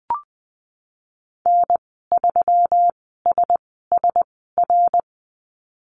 Morse code, heard as a keyed tone, about 19 wpm, spaced as for 14 wpm, 700 Hz. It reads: N3SSR
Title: 128 Lesson 21g - Number 6 - Call Signs 20wpm